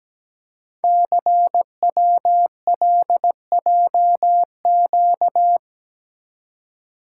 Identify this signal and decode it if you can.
Morse code, heard as a keyed tone, about 17 wpm, 700 Hz. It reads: CWLJQ
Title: Code Groups - 5 Letters - 17wpm